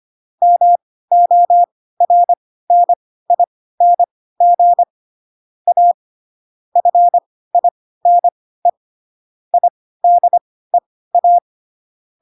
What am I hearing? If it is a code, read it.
Morse code, heard as a keyed tone, about 25 wpm, spaced as for 16 wpm, 700 Hz. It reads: MORNING A FINE IDEA
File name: Pinocchio / Pinocchio ch2 10 25 ik1hge